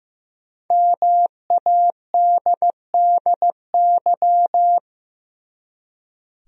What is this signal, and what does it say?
Morse code, heard as a keyed tone, about 15 wpm, 700 Hz. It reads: MADDY
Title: USA Common Names - 15wpm